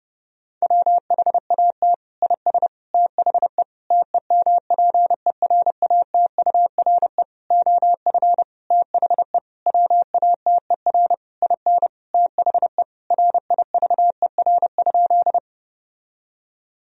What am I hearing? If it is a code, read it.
Morse code, heard as a keyed tone, about 30 wpm, 700 Hz. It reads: WHAT IS THE TEMPERATURE OF THE WATER IN THE RIVER?
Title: Sentences from Top 1000 Words - Review Part 4 - 30wpm